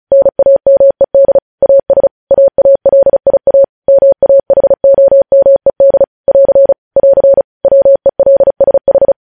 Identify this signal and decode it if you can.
Morse code, heard as a keyed tone, about 36 wpm, 550 Hz. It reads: NAMED AS AALIA MAHOMED <AR> <AR> WELSH
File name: podcast / News headlines-1x-35x35